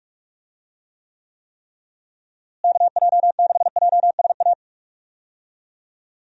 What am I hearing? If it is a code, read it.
Morse code, heard as a keyed tone, about 46 wpm, 700 Hz. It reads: KJ6JSA
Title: USA Call Signs - 6 Characters - Repeated 3x - 45wpm